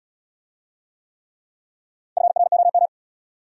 Morse code, heard as a keyed tone, about 67 wpm, 700 Hz. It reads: FROM